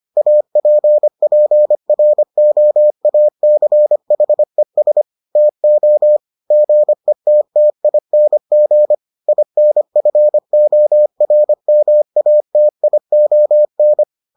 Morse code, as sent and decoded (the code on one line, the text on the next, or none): .- .--. .--. .-. --- .- -.-. .... ... / - --- / --. - - .. -. --. / .. -. ..-. --- .-. -- .- - .. --- -.
APPROACHES TO GETTING INFORMATION